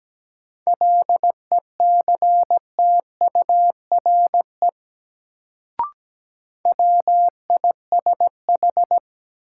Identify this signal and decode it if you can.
Morse code, heard as a keyed tone, about 17 wpm, 700 Hz. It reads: LECTURE WISH
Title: Sentences from Top 1600 Words - New Word Drill Alt - 17wpm